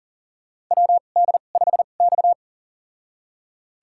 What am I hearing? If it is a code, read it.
Morse code, heard as a keyed tone, about 40 wpm, spaced as for 29 wpm, 690 Hz. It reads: WD5X
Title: USA Call Signs - 4 Characters - 2x Character Spacing - 40wpm